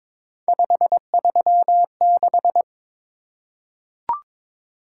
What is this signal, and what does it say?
Morse code, heard as a keyed tone, about 22 wpm, 700 Hz. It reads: 536
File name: practice / 22wpm / Three Letter-Number Combinations - 22wpm